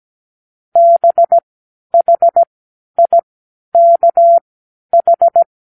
Morse code, heard as a keyed tone, about 17 wpm, spaced as for 10 wpm, 680 Hz. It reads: BHIKH
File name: k7qo / Lesson18(A-L)